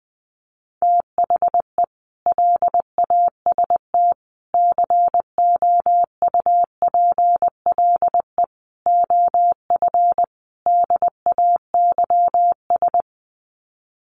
Morse code, heard as a keyed tone, about 20 wpm, 700 Hz. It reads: THE LAST COUPLE OF DAYS
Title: Sets of 5 Words Encore 20wpm